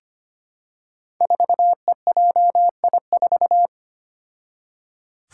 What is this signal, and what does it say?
Morse code, heard as a keyed tone, about 25 wpm, 700 Hz. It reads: 4EJI4
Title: Five Challenging Letter-Number Combinations 25wpm